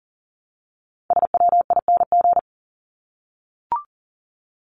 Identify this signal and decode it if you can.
Morse code, heard as a keyed tone, about 40 wpm, 700 Hz. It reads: SWING
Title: Top 1751-2000 Words 40wpm